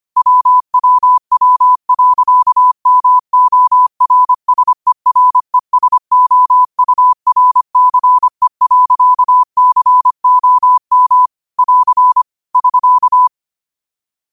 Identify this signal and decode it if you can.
Morse code, heard as a keyed tone, about 25 wpm, 1000 Hz. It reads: WWW.MORSERESOURCE.COM <AR> <SK>
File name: qotdfiles / 25wpm